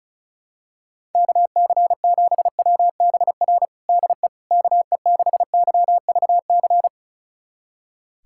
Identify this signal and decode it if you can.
Morse code, heard as a keyed tone, about 36 wpm, 700 Hz. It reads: KC7WBR DE KE6YVC
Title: USA Callsign DE Callsign - 6 Characters - 35wpm